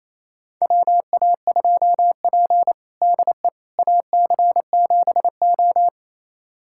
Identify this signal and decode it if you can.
Morse code, heard as a keyed tone, about 27 wpm, 700 Hz. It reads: WA2P DE AC7O
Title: USA Callsign DE Callsign - 4 Characters - 28wpm